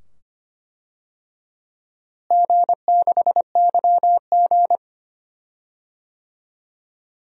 Morse code, heard as a keyed tone, about 25 wpm, 700 Hz. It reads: G6YG
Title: DXCC-100 Call Signs - 4 Characters - Repeated 2x - 25wpm